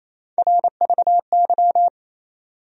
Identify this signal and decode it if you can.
Morse code, heard as a keyed tone, about 28 wpm, 700 Hz. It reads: RVY
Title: USA Call Sign Postfixes - 3 Letters 28wpm